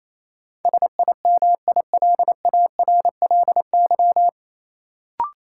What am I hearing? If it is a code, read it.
Morse code, heard as a keyed tone, about 28 wpm, 700 Hz. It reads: SIMILARLY
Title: Sentences from Top 1400 Words - New Word Drill - 28wpm